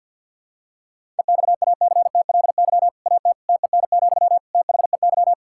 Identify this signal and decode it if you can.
Morse code, heard as a keyed tone, about 52 wpm, 700 Hz. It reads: EXACTLY AT TEN, THEY
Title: Sentences from Top 1000 Words - Review Part 6 - 50wpm